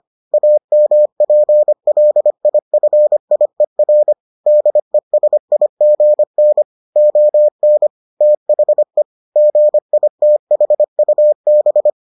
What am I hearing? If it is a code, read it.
Morse code, heard as a keyed tone, about 25 wpm, 600 Hz. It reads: AMPLIFIER DESIGN ON THE GITHUB